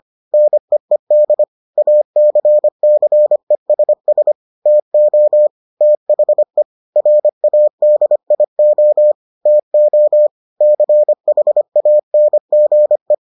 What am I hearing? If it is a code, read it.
Morse code, heard as a keyed tone, about 25 wpm, 600 Hz. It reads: NEED ACCESS TO THE RADIO TO CHANGE